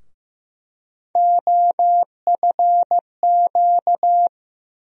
Morse code, heard as a keyed tone, about 15 wpm, 700 Hz. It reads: OFQ
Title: USA Call Sign Postfixes - 3 Letters 15wpm